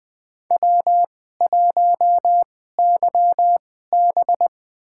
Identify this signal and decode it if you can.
Morse code, heard as a keyed tone, about 20 wpm, spaced as for 15 wpm, 700 Hz. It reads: W1YB
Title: USA Call Signs - 4 Characters - 2x Character Spacing - 20wpm